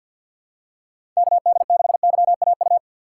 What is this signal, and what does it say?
Morse code, heard as a keyed tone, about 47 wpm, 690 Hz. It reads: KD6YAU
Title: USA Call Signs - 6 Characters - 3x Character Spacing - 50wpm